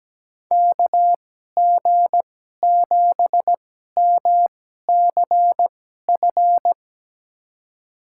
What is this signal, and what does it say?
Morse code, heard as a keyed tone, about 17 wpm, spaced as for 12 wpm, 700 Hz. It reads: KG7MCF
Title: USA Call Signs - 6 Characters - 2x Character Spacing - 17wpm